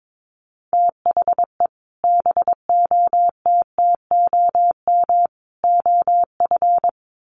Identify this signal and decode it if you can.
Morse code, heard as a keyed tone, about 22 wpm, 700 Hz. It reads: THE BOTTOM OF
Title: Sets of 3 Words 22wpm